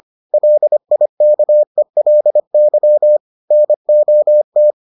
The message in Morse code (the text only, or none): LIKELY NOT